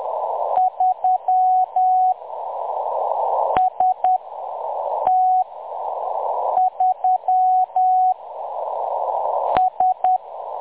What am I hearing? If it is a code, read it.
Morse code, heard as a keyed tone, about 10 wpm, spaced as for 6 wpm, 740 Hz. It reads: 3 ST3 S
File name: M23-ST3- -0858utc-10184khz